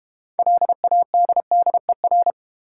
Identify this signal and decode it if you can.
Morse code, heard as a keyed tone, about 32 wpm, 700 Hz. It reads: LADDER